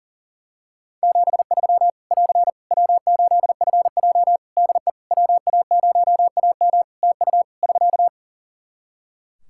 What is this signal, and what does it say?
Morse code, heard as a keyed tone, about 42 wpm, 700 Hz. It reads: 73 <AR> W8FJ DE WA0AM TU <SK>
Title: Practice QSO - Exchange 4 - Ending - 40wpm